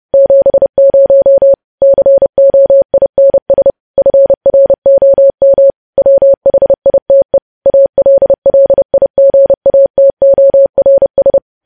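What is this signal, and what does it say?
Morse code, heard as a keyed tone, about 30 wpm, 550 Hz. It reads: 70 COINS FROM WHITE ALLIGATORS
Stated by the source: Morse code news headlines practice sent at 30 wpm.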